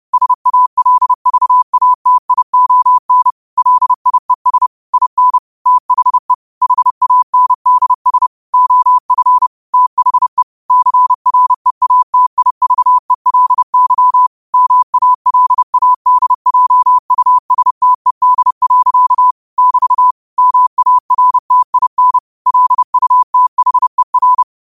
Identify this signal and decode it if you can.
Morse code, heard as a keyed tone, about 30 wpm, 1000 Hz. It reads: SALVATION LIES IN THE HANDS OF THE CREATIVELY MALADJUSTED. <BT> MARTIN LUTHER
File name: qotdfiles / 30wpm